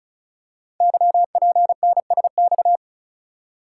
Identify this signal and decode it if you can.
Morse code, heard as a keyed tone, about 36 wpm, 700 Hz. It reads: YPNSX
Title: Code Groups - 5 Letter:Numbers - 35wpm